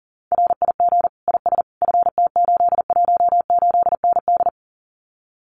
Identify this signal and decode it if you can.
Morse code, heard as a keyed tone, about 37 wpm, 700 Hz. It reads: RIG IS FT818ND
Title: Common QSO Phrases Encore 40wpm